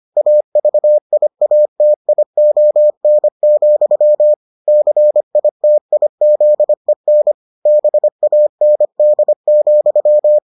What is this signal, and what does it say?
Morse code, heard as a keyed tone, about 25 wpm, 600 Hz. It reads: AVIATION, CITIZEN BAND,